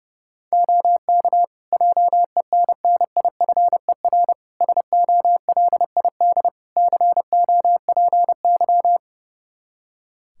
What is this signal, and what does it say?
Morse code, heard as a keyed tone, about 30 wpm, 700 Hz. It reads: OK JENNIFER SOLID COPY
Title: Practice QSO - Exchange 3 - More Info - 30wpm